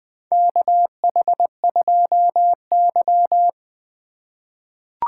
Morse code, heard as a keyed tone, about 20 wpm, 700 Hz. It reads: KH2Y
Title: USA Call Signs - 4 Characters - 4x Character Spacing - 20wpm